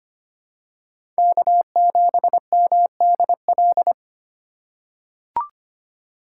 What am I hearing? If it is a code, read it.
Morse code, heard as a keyed tone, about 25 wpm, 700 Hz. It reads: K7MDL
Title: USA Call Signs - 5 Characters - 2x Character Spacing - 25wpm